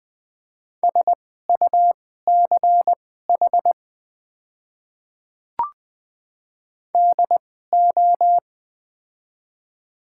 Morse code, heard as a keyed tone, about 20 wpm, spaced as for 14 wpm, 700 Hz. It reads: SUCH DO
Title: 2x Character Spacing - Top 200 Words - 20wpm